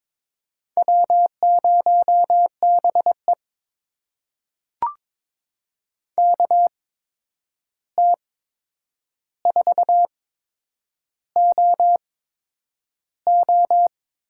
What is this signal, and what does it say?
Morse code, heard as a keyed tone, about 22 wpm, spaced as for 9 wpm, 700 Hz. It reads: W0BE KT4OO
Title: USA Call Signs - 8x Character Spacing - 22wpm